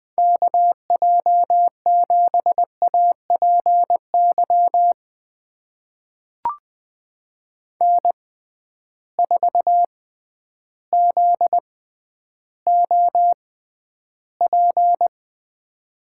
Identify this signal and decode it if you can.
Morse code, heard as a keyed tone, about 20 wpm, spaced as for 11 wpm, 700 Hz. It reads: KJ7APY N4ZOP
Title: USA Call Signs - 6x Character Spacing - 20wpm